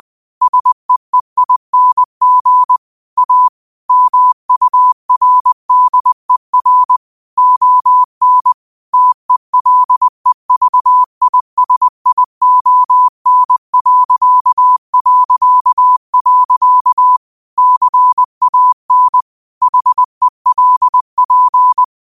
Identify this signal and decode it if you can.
Morse code, heard as a keyed tone, about 20 wpm, 1000 Hz. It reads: SEEING A MURDER ON TELEVISION... CAN HELP
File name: qotdfiles / 20wpm